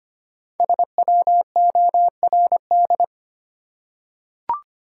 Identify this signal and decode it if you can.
Morse code, heard as a keyed tone, about 25 wpm, 700 Hz. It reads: SWORD
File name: 5 Letter Words - First 3 Letters Given - 25wpm